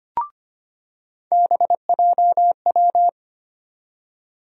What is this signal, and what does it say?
Morse code, heard as a keyed tone, about 25 wpm, 700 Hz. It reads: BJW
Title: USA Call Sign Postfixes - 3 Letters 25wpm